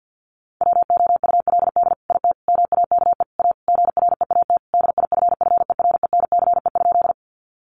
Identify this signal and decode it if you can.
Morse code, heard as a keyed tone, about 50 wpm, 700 Hz. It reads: WOULD IT MAKE A GREAT DIFFERENCE?
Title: Sentences from Top 800 Words 50wpm